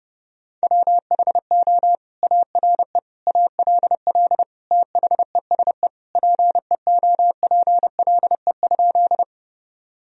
Morse code, heard as a keyed tone, about 32 wpm, 700 Hz. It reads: WHO ARE ALL THESE PEOPLE?
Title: Sentences from Top 1000 Words - Review Part 1 - 30wpm